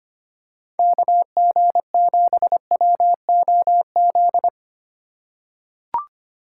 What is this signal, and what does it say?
Morse code, heard as a keyed tone, about 25 wpm, 700 Hz. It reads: KG7WOZ
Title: USA Call Signs - 2x Character Spacing - 25wpm